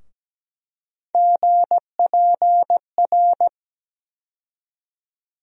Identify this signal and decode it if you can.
Morse code, heard as a keyed tone, about 17 wpm, 700 Hz. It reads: GPR
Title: USA Call Sign Postfixes - 3 Letters 17wpm